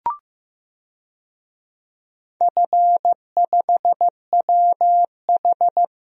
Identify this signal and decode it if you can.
Morse code, heard as a keyed tone, about 15 wpm, 700 Hz. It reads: F5WH
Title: DXCC-100 Call Signs - 4 Characters - 15wpm